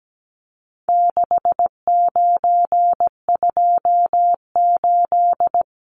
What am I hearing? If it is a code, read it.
Morse code, heard as a keyed tone, about 17 wpm, 700 Hz. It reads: 6928
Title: Numbers - 4-Digit 17wpm